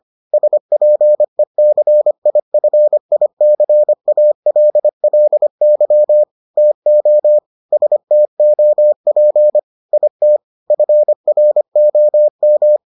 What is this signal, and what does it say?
Morse code, heard as a keyed tone, about 25 wpm, 600 Hz. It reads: SPECIFICALLY TO STOP IT FROM